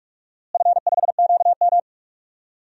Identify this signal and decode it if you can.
Morse code, heard as a keyed tone, about 44 wpm, 700 Hz. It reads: F5QM
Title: DXCC-100 Call Signs - 4 Characters - Repeated 2x - 45wpm